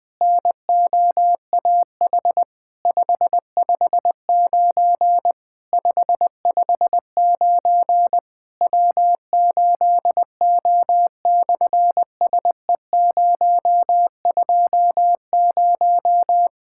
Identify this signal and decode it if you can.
Morse code, heard as a keyed tone, about 19 wpm, 700 Hz. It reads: NOAH 559 559 W8O/SE020
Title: SOTA QSO Practice 20wpm